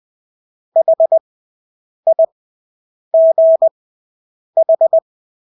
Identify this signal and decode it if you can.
Morse code, heard as a keyed tone, about 20 wpm, spaced as for 8 wpm, 650 Hz. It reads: HIGH
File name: Engelske ord 20/8 wpm